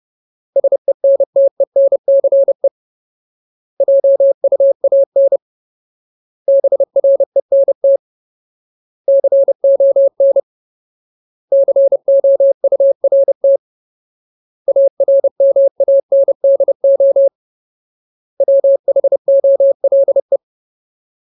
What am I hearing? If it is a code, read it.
Morse code, heard as a keyed tone, about 30 wpm, 550 Hz. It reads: SENTENCE JUAN BRENT CON COURT ARMANDO WHOLE
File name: Ultimate Word List / 30WPM Ultimate